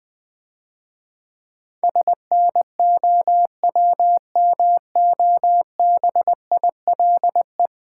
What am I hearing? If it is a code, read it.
Morse code, heard as a keyed tone, about 20 wpm, 700 Hz. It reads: SNOWMOBILE
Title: Difficult Words - Speed-Racing with Hints - 20wpm